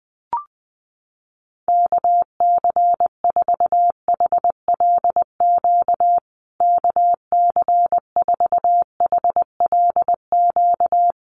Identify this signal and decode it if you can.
Morse code, heard as a keyed tone, about 20 wpm, 700 Hz. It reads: KC4HLQ KC4HLQ
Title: USA Call Signs Repeated 3x 20wpm